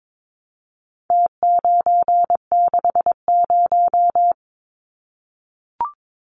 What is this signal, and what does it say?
Morse code, heard as a keyed tone, about 22 wpm, 700 Hz. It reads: T960
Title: Numbers - Contest Style - 4-Digit 22wpm